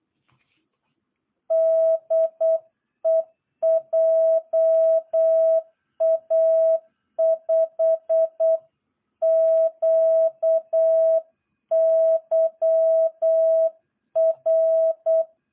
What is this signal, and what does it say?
Morse code, heard as a keyed tone, about 8 wpm, 660 Hz. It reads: DE JA5QYR
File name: jh8cbh / 2022cw